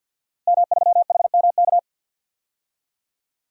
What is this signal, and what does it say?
Morse code, heard as a keyed tone, about 52 wpm, 700 Hz. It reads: M3HMK
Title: DXCC-100 Call Signs - 5 Characters - Repeated 2x - 50wpm